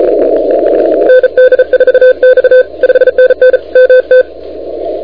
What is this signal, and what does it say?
Morse code, heard as a keyed tone, about 36 wpm, 1550 Hz, with noise. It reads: ND4X 5NN MT
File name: recordings / K0SN de NT1K ARRL10M CW 28MHz